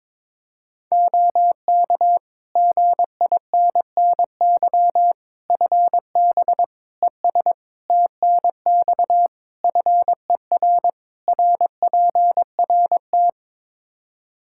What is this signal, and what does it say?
Morse code, heard as a keyed tone, about 22 wpm, 700 Hz. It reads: OK GINNY FB ES TNX FER RPRT
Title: Practice QSO - Exchange 2 - Info - 22wpm